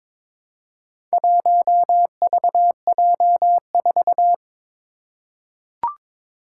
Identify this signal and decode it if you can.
Morse code, heard as a keyed tone, about 23 wpm, 700 Hz. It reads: 1VJ4
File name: practice / 22wpm / Four Letter-Number Combinations - 22wpm